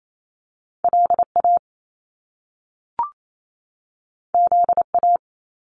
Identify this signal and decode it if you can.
Morse code, heard as a keyed tone, about 28 wpm, 700 Hz. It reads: LA ZA
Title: Top Two Letter Combinations in Words 28wpm